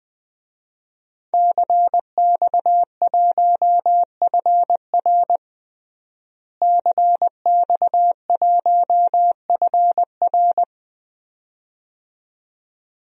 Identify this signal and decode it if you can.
Morse code, heard as a keyed tone, about 21 wpm, 700 Hz. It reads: CX1FR CX1FR
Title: DXCC-100 Call Signs - 5 Characters - Repeated 2x - 20wpm